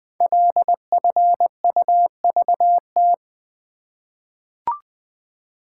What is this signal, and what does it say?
Morse code, heard as a keyed tone, about 20 wpm, 700 Hz. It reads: LFUVT
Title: Code Groups - 5 Letters - 20wpm